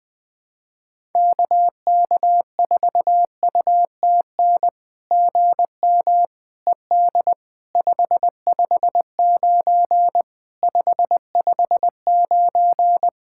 Morse code, heard as a keyed tone, about 20 wpm, 700 Hz. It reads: KK4UTN GM ED 559 559
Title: SOTA QSO Practice 20wpm